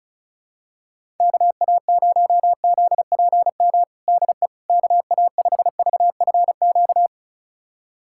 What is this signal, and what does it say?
Morse code, heard as a keyed tone, about 36 wpm, 700 Hz. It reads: KA0ZPM DE KA5VFQ
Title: USA Callsign DE Callsign - 6 Characters - 35wpm